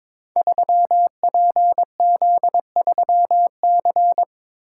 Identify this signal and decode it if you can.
Morse code, heard as a keyed tone, about 22 wpm, 700 Hz. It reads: 3PZ3C
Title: Code Groups - 5 Letter:Numbers - 22wpm